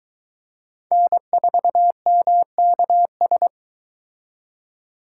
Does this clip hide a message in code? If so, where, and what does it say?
Yes: Morse code from 0.9 to 3.5 s, heard as a keyed tone, about 23 wpm, 700 Hz. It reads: N4MKS